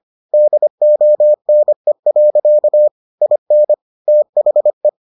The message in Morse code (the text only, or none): DONE. IN THE